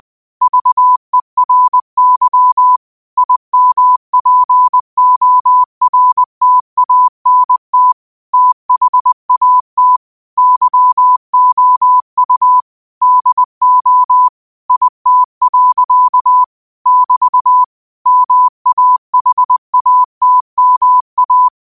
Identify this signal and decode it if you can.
Morse code, heard as a keyed tone, about 20 wpm, 1000 Hz. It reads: VERY IMPORTANT THAT YOU DO IT. <BT> MAHATMA
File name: qotdfiles / 20wpm